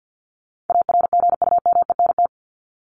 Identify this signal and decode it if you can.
Morse code, heard as a keyed tone, about 50 wpm, 700 Hz. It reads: ARGUMENT
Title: Top 1251-1500 Words 50wpm